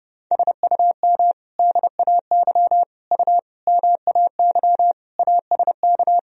Morse code, heard as a keyed tone, about 32 wpm, 700 Hz. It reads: SUM DAY U MAY ASK
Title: Sentences from Top 200 Words and Abbreviations 30wpm